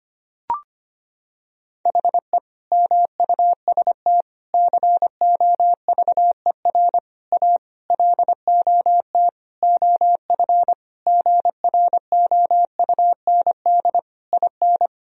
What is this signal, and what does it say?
Morse code, heard as a keyed tone, about 25 wpm, 700 Hz. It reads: HE MUST COVER A LOT OF GROUND IN
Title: Sentences from Top 1000 Words - Review Part 3 - 25wpm